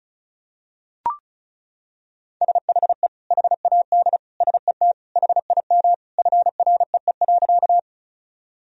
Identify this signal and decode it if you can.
Morse code, heard as a keyed tone, about 36 wpm, 700 Hz. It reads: SHE HAD SET HIM FREE.
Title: Sentences from Top 1000 Words - Review Part 1 - 35wpm